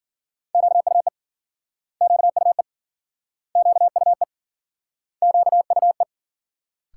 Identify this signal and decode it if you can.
Morse code, heard as a keyed tone, about 53 wpm, 700 Hz. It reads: QUE QUE QUE QUE